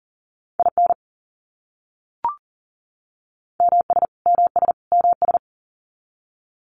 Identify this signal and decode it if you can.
Morse code, heard as a keyed tone, about 40 wpm, 700 Hz. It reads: IN MS MS MS
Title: US States Repeated 3x 40wpm